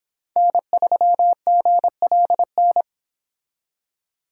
Morse code, heard as a keyed tone, about 25 wpm, 700 Hz. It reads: N3GLN